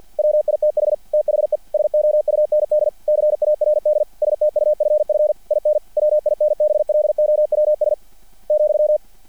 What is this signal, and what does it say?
Morse code, heard as a keyed tone, about 50 wpm, 600 Hz. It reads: WITH THE SOUND CARD STUFF IN WINBLOWS ,
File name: archived / CRSnet